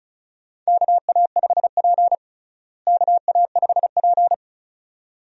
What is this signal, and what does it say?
Morse code, heard as a keyed tone, about 36 wpm, 700 Hz. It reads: KA5P KA5P
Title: USA Call Signs - 4 Characters - Repeated 2x - 35wpm